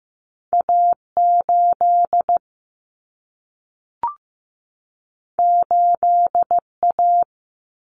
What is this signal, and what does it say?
Morse code, heard as a keyed tone, about 15 wpm, 700 Hz. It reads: A8 8A
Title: Common Cut Numbers - 2-Digit 15wpm